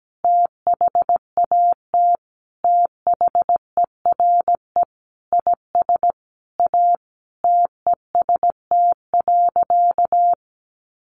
Morse code, heard as a keyed tone, about 17 wpm, 700 Hz. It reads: THAT THERE IS A TEST.
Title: Additional Easy Phrases 17wpm